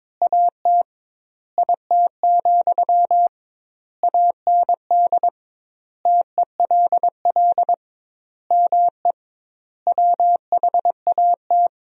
Morse code, heard as a keyed tone, about 21 wpm, 700 Hz. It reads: AT IT, AND TELL ME WHAT
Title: 2x Word-Spacing - Sentences from Top 200 Words - 22wpm